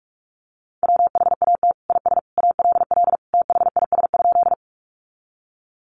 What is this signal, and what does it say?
Morse code, heard as a keyed tone, about 44 wpm, 700 Hz. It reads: WHAT IS ALL THIS?
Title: Sentences from Top 100 Words 45wpm